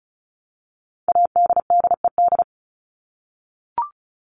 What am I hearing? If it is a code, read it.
Morse code, heard as a keyed tone, about 34 wpm, 700 Hz. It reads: ADDED
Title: ED Suffix - Top 100 Words with Base Words 35wpm